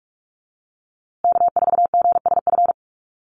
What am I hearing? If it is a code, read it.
Morse code, heard as a keyed tone, about 46 wpm, 700 Hz. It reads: K4GSF
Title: USA Call Signs Repeated 2x 45wpm